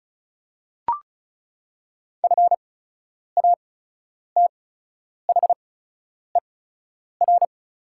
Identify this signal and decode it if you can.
Morse code, heard as a keyed tone, about 35 wpm, spaced as for 10 wpm, 690 Hz. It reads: FATHER
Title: 8x Character Spacing - Top 300 Words - 35wpm